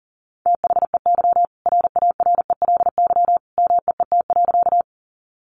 Morse code, heard as a keyed tone, about 40 wpm, 700 Hz. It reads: THEY RARELY MEET.
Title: Sentences from Top 1400 Words 40wpm